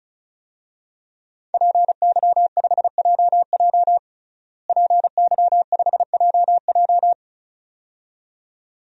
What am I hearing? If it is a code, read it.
Morse code, heard as a keyed tone, about 36 wpm, 700 Hz. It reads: PY5JJ PY5JJ
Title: DXCC-100 Call Signs - 5 Characters - Repeated 2x - 35wpm